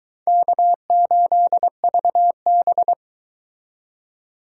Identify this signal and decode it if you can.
Morse code, heard as a keyed tone, about 24 wpm, 700 Hz. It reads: K8VB